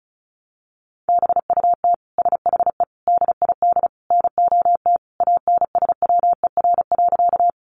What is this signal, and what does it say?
Morse code, heard as a keyed tone, about 34 wpm, 700 Hz. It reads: BUT SHE DID NOT ANSWER.
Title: Sentences from Top 600 Words 35wpm